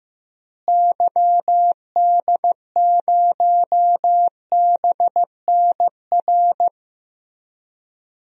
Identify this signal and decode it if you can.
Morse code, heard as a keyed tone, about 15 wpm, 700 Hz. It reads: YD0BNR
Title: DXCC-100 Call Signs - 6 Characters - Repeated 3x - 15wpm